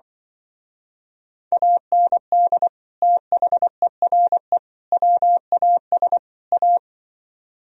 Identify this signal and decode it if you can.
Morse code, heard as a keyed tone, about 25 wpm, 700 Hz. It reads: AND THERE WAS A